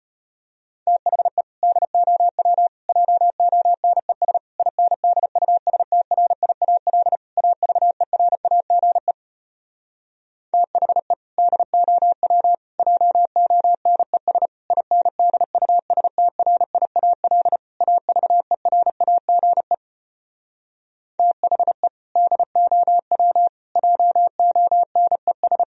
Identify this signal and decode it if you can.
Morse code, heard as a keyed tone, about 33 wpm, 700 Hz. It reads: THE DOW JONES INDUSTRIAL AVERAGE THE DOW JONES INDUSTRIAL AVERAGE THE DOW JONES